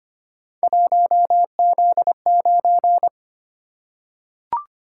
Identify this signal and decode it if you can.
Morse code, heard as a keyed tone, about 25 wpm, 700 Hz. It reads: 1Z9
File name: Three Letter-Number Combinations - 25wpm